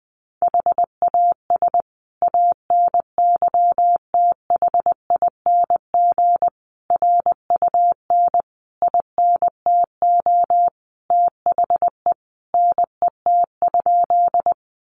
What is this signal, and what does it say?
Morse code, heard as a keyed tone, about 20 wpm, 700 Hz. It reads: HAS ANYTHING RUN INTO THE NET?
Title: Sentences from Top 1400 Words 20wpm